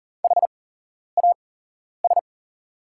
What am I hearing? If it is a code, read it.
Morse code, heard as a keyed tone, about 40 wpm, spaced as for 11 wpm, 690 Hz. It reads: HAS
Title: 8x Character Spacing - Top 200 Words - 40wpm